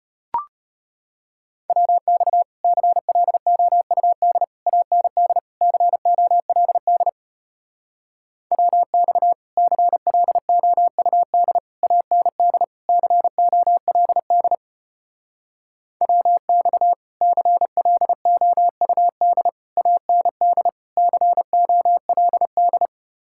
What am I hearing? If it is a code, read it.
Morse code, heard as a keyed tone, about 33 wpm, 700 Hz. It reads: WX CLOUD AND COLD WX CLOUD AND COLD WX CLOUD AND COLD